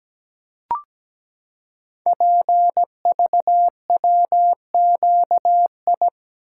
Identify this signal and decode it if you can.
Morse code, heard as a keyed tone, about 17 wpm, 700 Hz. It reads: PVWQI
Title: Code Groups - 5 Letters - 17wpm